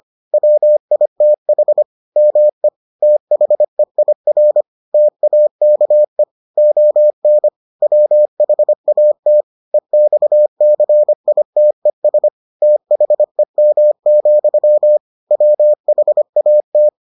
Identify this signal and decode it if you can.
Morse code, heard as a keyed tone, about 25 wpm, 600 Hz. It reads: WITH ME THEIR TAKE ON WHAT EXCITES THEM, WHAT